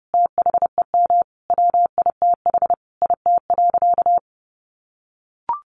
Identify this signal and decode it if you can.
Morse code, heard as a keyed tone, about 30 wpm, 700 Hz. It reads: THEM WITH IT.
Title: A Sound Of Thunder 30wpm